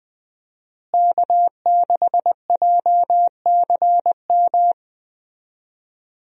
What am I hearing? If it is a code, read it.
Morse code, heard as a keyed tone, about 21 wpm, 700 Hz. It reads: K6JCM
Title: USA Call Signs - 5 Characters - Repeated 2x - 20wpm